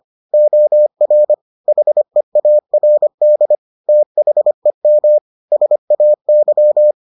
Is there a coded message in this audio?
Yes: Morse code from 0.3 to 6.9 s, heard as a keyed tone, about 25 wpm, 600 Hz. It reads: OR HEARD THEM SAY